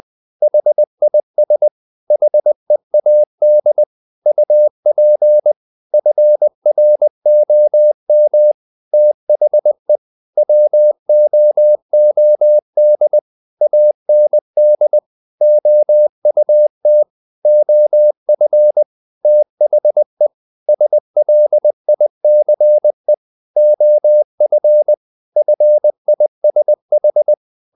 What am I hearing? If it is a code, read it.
Morse code, heard as a keyed tone, about 20 wpm, 600 Hz. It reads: HIS HEAD UP FROM THE WOOD AND OUT OF THE SLICE OF FISH
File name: the old man and the sea part 2 20wpm